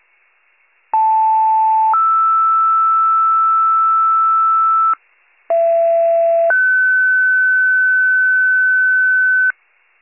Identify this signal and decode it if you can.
Two-tone page, heard as a one-second tone, then a three-second tone, about 880 then 1320 Hz, about 670 then 1500 Hz, with noise. It is heard from 0.9 to 9.5 s.